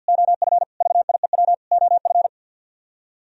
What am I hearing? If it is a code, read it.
Morse code, heard as a keyed tone, about 48 wpm, 700 Hz. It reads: OF VIEW OF
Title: Speed-Racing - Sets of 5 Words 50wpm